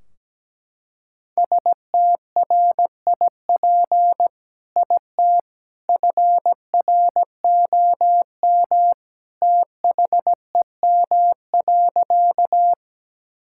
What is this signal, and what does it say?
Morse code, heard as a keyed tone, about 17 wpm, 700 Hz. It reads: STRIP IT FROM THEM.
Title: Sentences from Top 1600 Words 17wpm